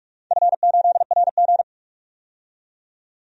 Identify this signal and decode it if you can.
Morse code, heard as a keyed tone, about 41 wpm, 700 Hz. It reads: F8RG
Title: DXCC-100 Call Signs - 4 Characters - 45wpm